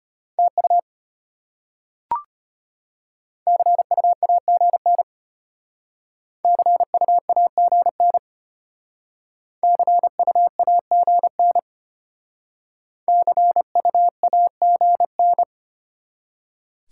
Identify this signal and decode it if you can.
Morse code, heard as a keyed tone, about 34 wpm, 700 Hz. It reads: TU CU AGN CU AGN CU AGN CU AGN